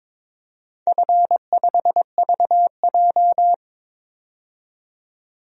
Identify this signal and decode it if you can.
Morse code, heard as a keyed tone, about 22 wpm, 700 Hz. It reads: F5VJ
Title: DXCC-100 Call Signs - 4 Characters - 22wpm